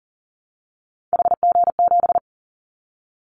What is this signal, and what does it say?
Morse code, heard as a keyed tone, about 41 wpm, 700 Hz. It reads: HG7
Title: Callsign Prefixes for Top 50 DXCC - 40wpm